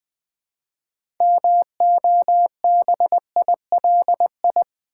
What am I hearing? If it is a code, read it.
Morse code, heard as a keyed tone, about 20 wpm, 700 Hz. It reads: MOBILI
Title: Build A Word - 3 Letters At A Time - 8-Letter Words - 20wpm